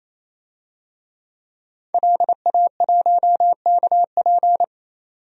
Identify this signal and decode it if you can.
Morse code, heard as a keyed tone, about 28 wpm, 700 Hz. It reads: LA1KP
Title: DXCC-100 Call Signs - 5 Characters - 28wpm